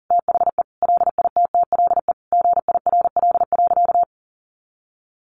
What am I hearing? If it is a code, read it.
Morse code, heard as a keyed tone, about 40 wpm, 700 Hz. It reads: THE LITTLE GIRL.
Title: Sentences from Top 400 Words 40wpm